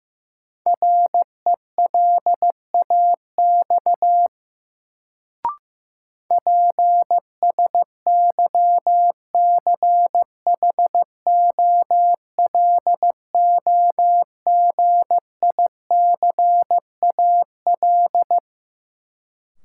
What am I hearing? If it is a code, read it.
Morse code, heard as a keyed tone, about 15 wpm, 700 Hz. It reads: RELAX PSYCHOLOGICAL
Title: Sentences from Top 1400 Words - New Word Drill - 15wpm